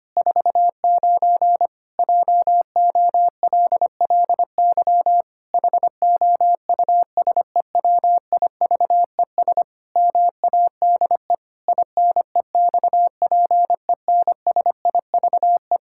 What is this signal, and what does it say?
Morse code, heard as a keyed tone, about 25 wpm, 700 Hz. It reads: 49 JOLLY HOUSEWIVES MADE INEXPENSIVE
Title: Pangram Challenge v2 - 25wpm